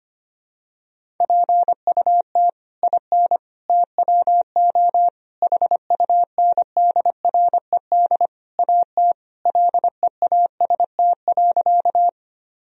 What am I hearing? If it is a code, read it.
Morse code, heard as a keyed tone, about 25 wpm, 700 Hz. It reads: PUT IN TWO HUNDRED AT LEAST.
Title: Sentences from Top 1000 Words - Review Part 2 - 25wpm